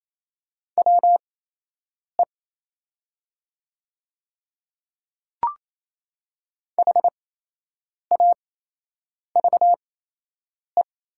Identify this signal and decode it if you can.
Morse code, heard as a keyed tone, about 28 wpm, spaced as for 8 wpm, 700 Hz. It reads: WE HAVE
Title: 8x Character Spacing - Top 100 Words - 28wpm